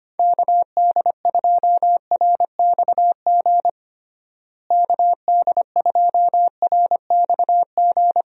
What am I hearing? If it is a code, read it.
Morse code, heard as a keyed tone, about 25 wpm, 700 Hz. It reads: KD2RXG KD2RXG
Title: USA Call Signs - 6 Characters - Repeated 2x - 25wpm